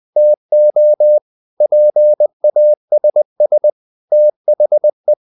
Morse code, heard as a keyed tone, about 20 wpm, 600 Hz. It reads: TO PASS THE
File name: the old man and the sea part 2 20wpm